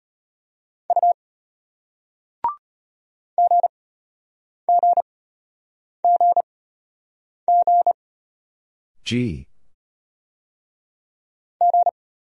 Morse code, heard as a keyed tone, about 33 wpm, spaced as for 8 wpm, 700 Hz. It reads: U GGGG G